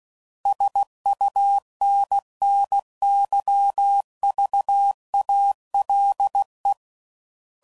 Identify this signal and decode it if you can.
Morse code, heard as a keyed tone, about 16 wpm, 800 Hz. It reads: SUNNYVALE
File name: CW Code Groups @15WPM